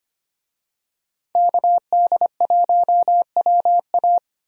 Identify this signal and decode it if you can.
Morse code, heard as a keyed tone, about 25 wpm, 700 Hz. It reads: KD1WA
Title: USA Call Signs - 5 Characters - 25wpm